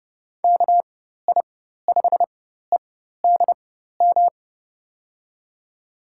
Morse code, heard as a keyed tone, about 30 wpm, spaced as for 14 wpm, 700 Hz. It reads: KI5EDM
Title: USA Call Signs - 6 Characters - 4x Character Spacing - 30wpm